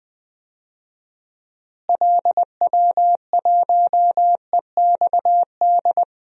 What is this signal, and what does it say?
Morse code, heard as a keyed tone, about 20 wpm, 700 Hz. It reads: LW1EXD
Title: DXCC-100 Call Signs - 6 Characters - Repeated 2x - 20wpm